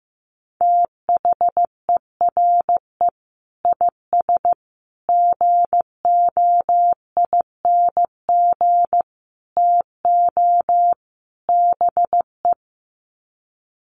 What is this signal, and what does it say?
Morse code, heard as a keyed tone, about 15 wpm, 700 Hz. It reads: THERE IS GOING TO BE
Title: Sets of 5 Words Encore 15wpm